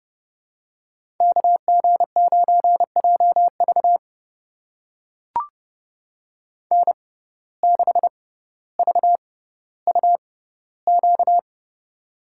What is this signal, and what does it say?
Morse code, heard as a keyed tone, about 30 wpm, spaced as for 15 wpm, 690 Hz. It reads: KG9JV N6VUQ
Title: USA Call Signs - 5 Characters - 6x Character Spacing - 30wpm